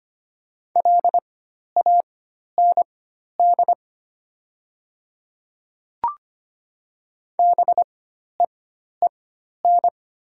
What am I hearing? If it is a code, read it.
Morse code, heard as a keyed tone, about 25 wpm, spaced as for 12 wpm, 700 Hz. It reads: LAND BEEN
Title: 4x Character Spacing - Top 200 Words - 25wpm